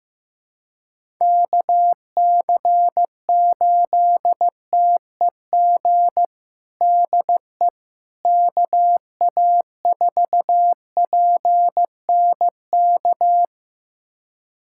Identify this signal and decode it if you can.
Morse code, heard as a keyed tone, about 15 wpm, 700 Hz. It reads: KC8TEG DE KA4PNK
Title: USA Callsign DE Callsign 15wpm